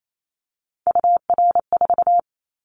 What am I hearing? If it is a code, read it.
Morse code, heard as a keyed tone, about 28 wpm, 700 Hz. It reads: UR4
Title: Callsign Prefixes for Top 25 DXCC - 28wpm